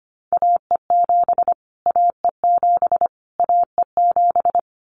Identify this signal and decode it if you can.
Morse code, heard as a keyed tone, about 25 wpm, 700 Hz. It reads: AE7 AE7 AE7
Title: Callsign Prefixes for Top 25 DXCC - Repeated 3x - 25wpm